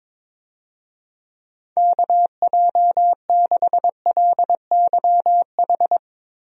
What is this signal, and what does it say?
Morse code, heard as a keyed tone, about 22 wpm, 700 Hz. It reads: KJ6LYH
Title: USA Call Signs - 6 Characters - Repeated 2x - 22wpm